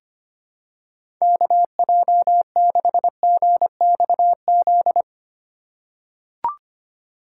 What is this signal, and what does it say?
Morse code, heard as a keyed tone, about 25 wpm, 700 Hz. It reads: KJ6GXZ
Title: USA Call Signs - 6 Characters - 8x Character Spacing - 25wpm